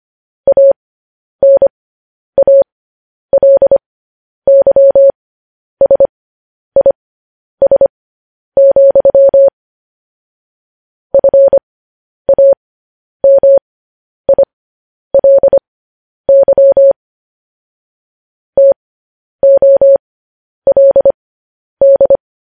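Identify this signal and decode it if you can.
Morse code, heard as a keyed tone, about 25 wpm, spaced as for 10 wpm, 550 Hz. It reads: ANALYSIS, FAMILY TOLD
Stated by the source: Morse code sending of news headlines, speed 10.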